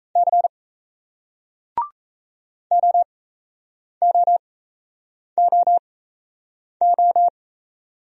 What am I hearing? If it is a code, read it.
Morse code, heard as a keyed tone, about 38 wpm, spaced as for 8 wpm, 700 Hz. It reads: C OOOO